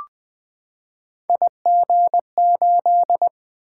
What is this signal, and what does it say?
Morse code, heard as a keyed tone, about 20 wpm, 700 Hz. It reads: IG8
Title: Three Letter-Number Combinations - 20wpm